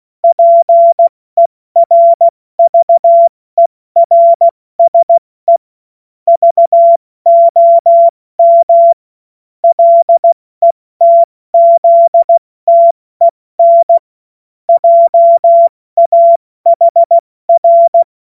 Morse code, heard as a keyed tone, about 16 wpm, spaced as for 14 wpm, 680 Hz. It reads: PERVERSE VOM LETZTEN JAHR
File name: Morsekurs / Witze / Witz 16. November 2021, 70 F80